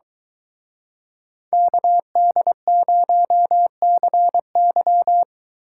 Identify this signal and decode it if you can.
Morse code, heard as a keyed tone, about 23 wpm, 700 Hz. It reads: KD0CY